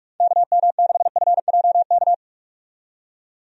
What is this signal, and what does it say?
Morse code, heard as a keyed tone, about 46 wpm, 700 Hz. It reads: KM6FJK
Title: USA Call Signs - 6 Characters - 45wpm